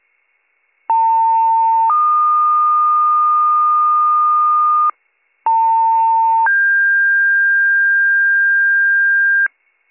Two-tone page, heard as a one-second tone, then a three-second tone, about 900 then 1220 Hz, about 900 then 1600 Hz, with noise. It is heard from 0.9 to 9.5 s.